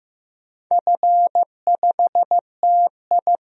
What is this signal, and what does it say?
Morse code, heard as a keyed tone, about 16 wpm, 700 Hz. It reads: F5TI
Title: DXCC-100 Call Signs - 4 Characters - Repeated 3x - 15wpm